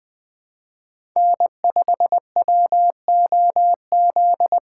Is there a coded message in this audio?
Yes: Morse code from 1.2 to 4.6 s, heard as a keyed tone, about 20 wpm, 700 Hz. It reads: N5WOZ